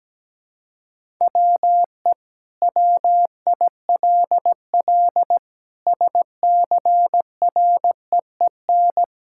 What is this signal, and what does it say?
Morse code, heard as a keyed tone, about 17 wpm, 700 Hz. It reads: WE WILL SCREEN
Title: Sentences from Top 1000 Words - Review Part 6 - 17wpm